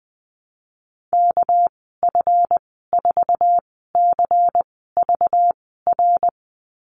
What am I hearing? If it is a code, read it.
Morse code, heard as a keyed tone, about 20 wpm, spaced as for 14 wpm, 700 Hz. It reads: KF4CVR
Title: 213 Lesson 34g - Number 8 - Call Signs 20wpm